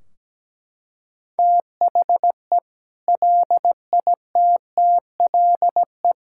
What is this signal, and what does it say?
Morse code, heard as a keyed tone, about 17 wpm, 700 Hz. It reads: THE LITTLE
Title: Sentences from Top 1600 Words 17wpm